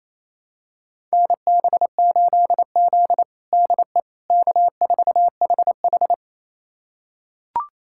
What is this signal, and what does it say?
Morse code, heard as a keyed tone, about 28 wpm, 700 Hz. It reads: NB8Z DE K4HH
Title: USA Callsign DE Callsign - 4 Characters - 28wpm